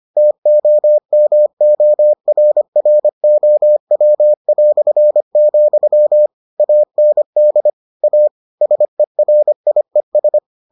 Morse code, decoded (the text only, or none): TOMORROW", AND A SERIES